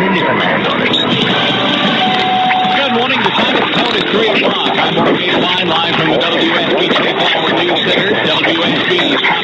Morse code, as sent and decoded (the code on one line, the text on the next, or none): none